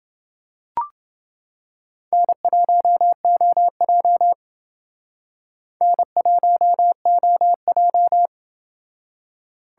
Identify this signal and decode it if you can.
Morse code, heard as a keyed tone, about 28 wpm, 700 Hz. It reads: N1OJ N1OJ